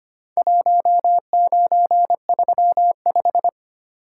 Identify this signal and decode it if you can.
Morse code, heard as a keyed tone, about 25 wpm, 700 Hz. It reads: 1935
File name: practice / 25wpm / Four Letter-Number Combinations - 25wpm